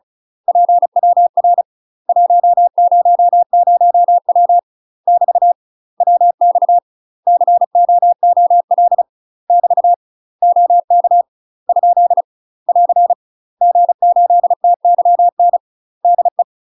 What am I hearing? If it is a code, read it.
Morse code, heard as a keyed tone, about 35 wpm, 700 Hz. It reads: PWR 100W <BT> WX COOL <BT> OK ? <AR> G8TYN DE
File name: UK MASTER QSOs TRAINING / 35WPM / UK MASTER QSOs 0029